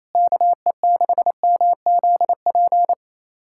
Morse code, heard as a keyed tone, about 28 wpm, 700 Hz. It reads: KE6MZP
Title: USA Callsign DE Callsign - 6 Characters - 28wpm